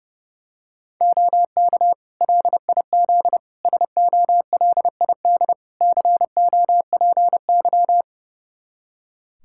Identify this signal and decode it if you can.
Morse code, heard as a keyed tone, about 32 wpm, 700 Hz. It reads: OK LIZ SOLID COPY
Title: K1USN SST QSO Practice 30wpm